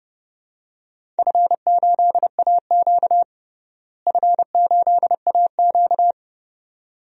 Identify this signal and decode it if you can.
Morse code, heard as a keyed tone, about 30 wpm, 700 Hz. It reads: F8AQ F8AQ
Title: DXCC-100 Call Signs - 4 Characters - Repeated 2x - 30wpm